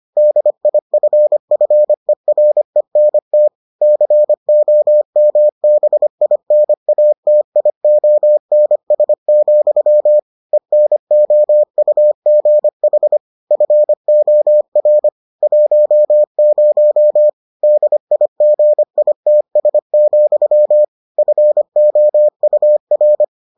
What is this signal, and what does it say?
Morse code, heard as a keyed tone, about 25 wpm, 600 Hz. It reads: DIFFERENT COMBINATIONS, ENOUGH FOR 10 DIGITS, FOUR